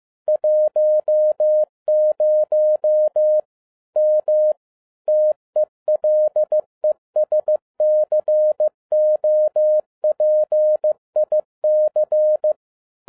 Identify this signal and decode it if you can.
Morse code, heard as a keyed tone, about 15 wpm, 610 Hz. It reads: 10 M TELESCOPIC
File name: 2011-10-mp3 / n12-JL3YMV 15wpm